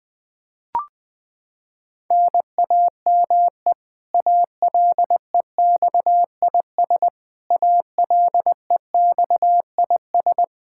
Morse code, heard as a keyed tone, about 20 wpm, 700 Hz. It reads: NAME ALEXIS ALEXIS
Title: Practice QSO - Exchange 1 - Beginning - 20wpm